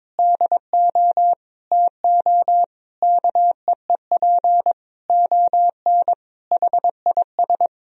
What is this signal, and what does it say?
Morse code, heard as a keyed tone, about 22 wpm, 700 Hz. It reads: DO TO KEEP ON HIS
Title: Sentences from Top 1000 Words - Review Part 1 - 22wpm